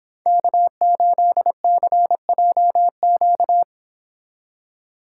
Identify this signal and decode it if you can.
Morse code, heard as a keyed tone, about 31 wpm, 700 Hz. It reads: K8CJQ